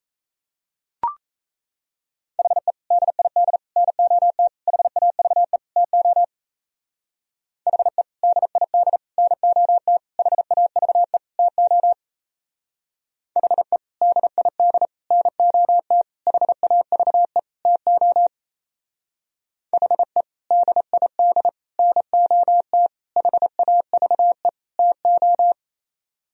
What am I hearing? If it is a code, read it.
Morse code, heard as a keyed tone, about 35 wpm, 700 Hz. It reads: HE DID NOT HAVE TO HE DID NOT HAVE TO HE DID NOT HAVE TO HE DID NOT HAVE TO
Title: Speed-Racing - Sets of 5 Words 28wpm